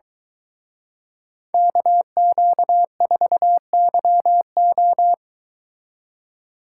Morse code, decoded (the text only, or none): KQ4YO